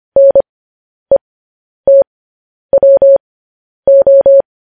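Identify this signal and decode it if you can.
Morse code, heard as a keyed tone, about 25 wpm, spaced as for 10 wpm, 550 Hz. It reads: NETWO
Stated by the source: Morse code news headlines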